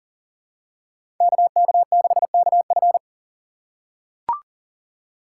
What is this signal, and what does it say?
Morse code, heard as a keyed tone, about 39 wpm, 700 Hz. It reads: KK6KF
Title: USA Call Signs - 5 Characters - 3x Character Spacing - 40wpm